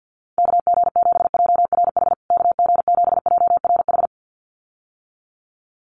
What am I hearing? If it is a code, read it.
Morse code, heard as a keyed tone, about 50 wpm, 700 Hz. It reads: KG7JRH KG7JRH
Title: USA Call Signs Repeated 2x 50wpm